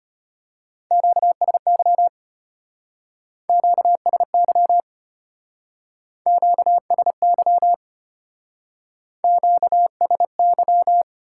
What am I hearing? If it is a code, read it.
Morse code, heard as a keyed tone, about 32 wpm, 700 Hz. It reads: QSY QSY QSY QSY